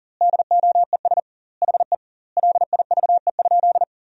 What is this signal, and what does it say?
Morse code, heard as a keyed tone, about 40 wpm, 700 Hz. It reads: DOES HE LIVE?
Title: 2x Word-Spacing - Sentences from Top 200 Words - 40wpm